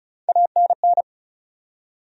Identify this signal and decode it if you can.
Morse code, heard as a keyed tone, about 34 wpm, 700 Hz. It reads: ANN
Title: USA Common Names - Max 5 Letters - 35wpm